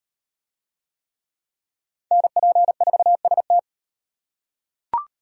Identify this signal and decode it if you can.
Morse code, heard as a keyed tone, about 34 wpm, 700 Hz. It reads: NP4ST